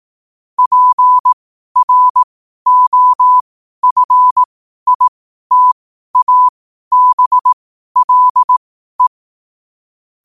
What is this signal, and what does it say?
Morse code, heard as a keyed tone, about 18 wpm, spaced as for 12 wpm, 1000 Hz. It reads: PROFITABLE
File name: qotdfiles / 12wpm